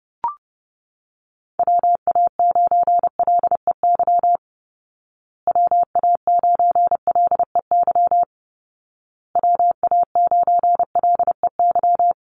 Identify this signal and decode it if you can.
Morse code, heard as a keyed tone, about 28 wpm, 700 Hz. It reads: WA9LEY WA9LEY WA9LEY
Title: Call Signs Repeated 2x - 30wpm